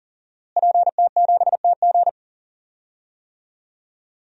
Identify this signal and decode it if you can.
Morse code, heard as a keyed tone, about 38 wpm, 700 Hz. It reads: PT7TG
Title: DXCC-100 Call Signs - 5 Characters - Repeated 2x - 40wpm